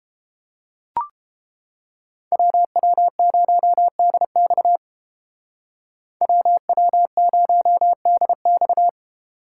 Morse code, heard as a keyed tone, about 29 wpm, 700 Hz. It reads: WW0DX WW0DX